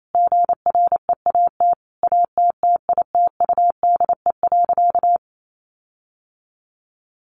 Morse code, heard as a keyed tone, about 28 wpm, 700 Hz. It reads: GREAT ATTITUDE.
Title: Sentences from Top 1300 Words 28wpm